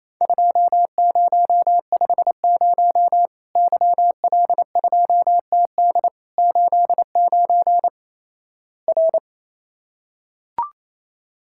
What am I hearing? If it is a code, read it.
Morse code, heard as a keyed tone, about 28 wpm, 700 Hz. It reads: 2050 YL2TD 89 R
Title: Worked All Europe DX Contest - QTC Exchange - 28wpm